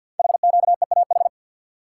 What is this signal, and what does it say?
Morse code, heard as a keyed tone, about 49 wpm, 700 Hz. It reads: HQEAH
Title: Code Groups - 5 Letters - 50wpm